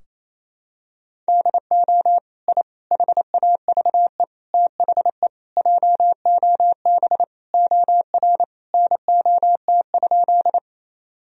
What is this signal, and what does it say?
Morse code, heard as a keyed tone, about 28 wpm, 700 Hz. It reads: DO I HAVE THE JOB OR NOT?
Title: Sentences from Top 1000 Words - Review Part 3 - 28wpm